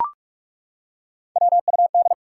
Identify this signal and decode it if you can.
Morse code, heard as a keyed tone, about 42 wpm, 700 Hz. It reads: WUD
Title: Top 180 QSO Elements 45wpm